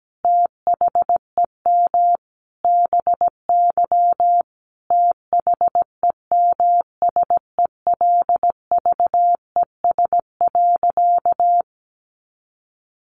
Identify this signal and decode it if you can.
Morse code, heard as a keyed tone, about 17 wpm, 700 Hz. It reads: THEM BY THEMSELVES.
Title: Sentences from Top 800 Words 17wpm